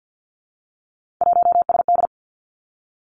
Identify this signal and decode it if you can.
Morse code, heard as a keyed tone, about 50 wpm, 700 Hz. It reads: 1SD
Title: Three Challenging Letter-Number Combinations 50wpm